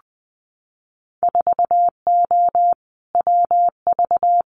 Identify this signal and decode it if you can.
Morse code, heard as a keyed tone, about 20 wpm, 700 Hz. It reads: 4O WV
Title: Winter Field Day Exchanges Encore 20wpm